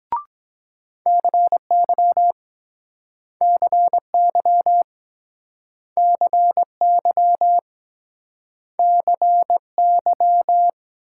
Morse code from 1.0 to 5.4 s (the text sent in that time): CY CY